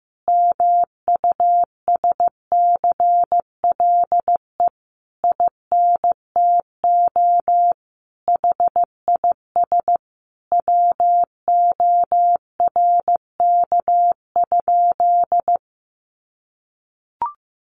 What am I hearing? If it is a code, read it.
Morse code, heard as a keyed tone, about 15 wpm, 700 Hz. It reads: MUSCLE INTO HIS WORK?
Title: Sentences from Top 1300 Words 15wpm